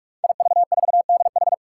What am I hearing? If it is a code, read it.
Morse code, heard as a keyed tone, about 46 wpm, 700 Hz. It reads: IV4DH
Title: Five Challenging Letter-Number Combinations 45wpm